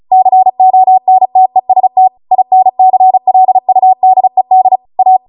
Morse code, heard as a keyed tone, about 36 wpm, 750 Hz. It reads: CONTEST INCLUDED A